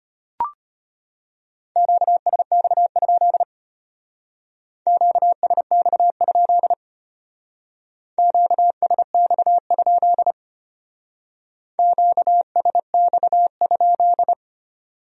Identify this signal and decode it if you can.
Morse code, heard as a keyed tone, about 30 wpm, 700 Hz. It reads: QSX? QSX? QSX? QSX?
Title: Speed-Racing Q-Codes - 25wpm